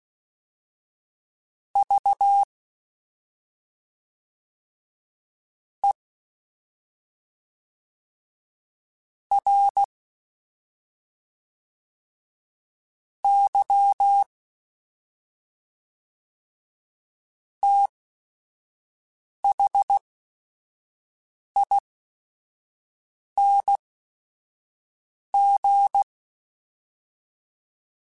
Morse code, heard as a keyed tone, about 16 wpm, spaced as for 5 wpm, 800 Hz. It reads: V E R Y THING
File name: Special INGs / Special INGs-mp3s / SlowToFaster ing @5